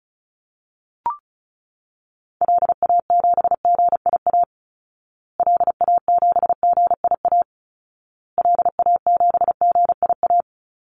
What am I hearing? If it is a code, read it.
Morse code, heard as a keyed tone, about 34 wpm, 700 Hz. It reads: LA7GIA LA7GIA LA7GIA
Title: Call Signs Repeated 2x - 35wpm